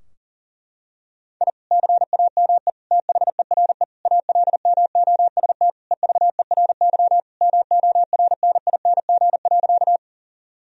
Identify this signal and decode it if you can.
Morse code, heard as a keyed tone, about 40 wpm, 700 Hz. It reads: I CAME THERE ALMOST EVERY MORNING.
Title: Sentences from Top 1000 Words - Review Part 4 - 40wpm